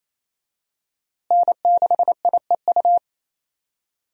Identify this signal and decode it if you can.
Morse code, heard as a keyed tone, about 28 wpm, 700 Hz. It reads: N6IEU
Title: USA Call Signs - 5 Characters - Repeated 2x - 28wpm